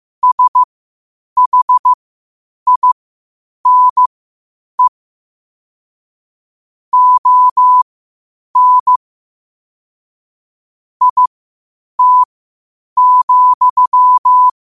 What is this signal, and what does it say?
Morse code, heard as a keyed tone, about 15 wpm, spaced as for 8 wpm, 1000 Hz. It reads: SHINE ON IT,
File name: qotdfiles / 8wpm